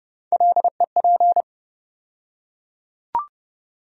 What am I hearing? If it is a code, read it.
Morse code, heard as a keyed tone, about 30 wpm, 700 Hz. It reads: LEP
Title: USA Call Sign Postfixes - 3 Letters 30wpm